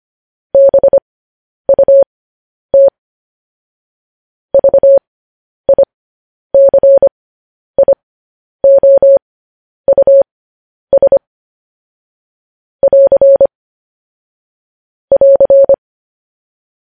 Morse code, heard as a keyed tone, about 25 wpm, spaced as for 10 wpm, 550 Hz. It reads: BUT VICIOUS <AR> <AR>